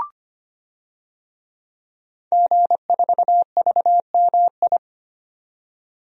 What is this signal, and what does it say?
Morse code, heard as a keyed tone, about 25 wpm, 700 Hz. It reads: G4VMI
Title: DXCC-100 Call Signs - 5 Characters - 25wpm